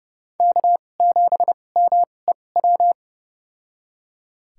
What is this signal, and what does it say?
Morse code, heard as a keyed tone, about 32 wpm, spaced as for 22 wpm, 700 Hz. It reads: K7MEW
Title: USA Call Signs - 5 Characters - 2x Character Spacing - 30wpm